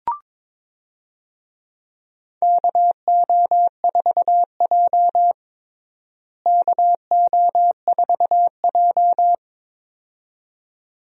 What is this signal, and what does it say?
Morse code, heard as a keyed tone, about 22 wpm, 700 Hz. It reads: KO4J KO4J
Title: USA Call Signs - 4 Characters - Repeated 2x - 22wpm